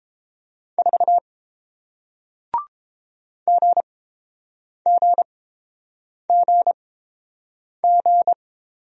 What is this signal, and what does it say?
Morse code, heard as a keyed tone, about 30 wpm, spaced as for 7 wpm, 700 Hz. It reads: 4 GGGG